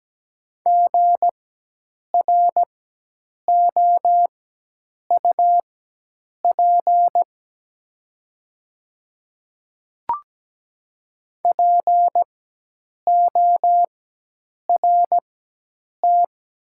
Morse code, heard as a keyed tone, about 17 wpm, spaced as for 8 wpm, 700 Hz. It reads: GROUP PORT
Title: 4x Character Spacing - Top 300 Words - 17wpm